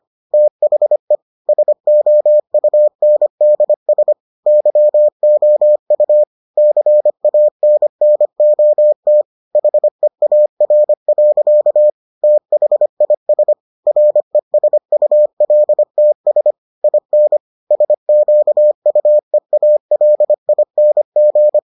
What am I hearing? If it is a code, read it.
Morse code, heard as a keyed tone, about 25 wpm, 600 Hz. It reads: THE SOUNDS YOU CANNOT HEAR. THIS RESULTS IN SQUEALING